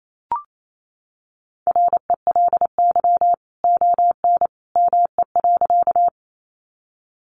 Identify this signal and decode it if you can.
Morse code, heard as a keyed tone, about 28 wpm, 700 Hz. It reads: RELY ON ME.
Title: Sentences from Top 1200 Words 28wpm